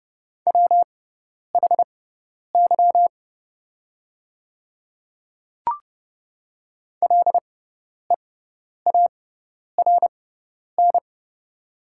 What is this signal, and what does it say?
Morse code, heard as a keyed tone, about 30 wpm, spaced as for 10 wpm, 700 Hz. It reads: WHY LEARN
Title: 6x Character Spacing - Top 500 Words - 30wpm